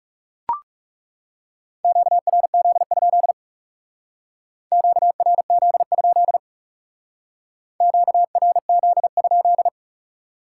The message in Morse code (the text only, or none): QRZ? QRZ? QRZ?